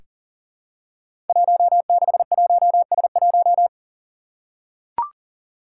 Morse code, heard as a keyed tone, about 40 wpm, 700 Hz. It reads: 161S1
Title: Five Challenging Letter-Number Combinations 40wpm